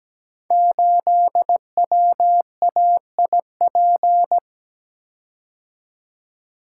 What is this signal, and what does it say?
Morse code, heard as a keyed tone, about 17 wpm, 700 Hz. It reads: 8WAIP
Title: Code Groups - 5 Letter:Numbers - 17wpm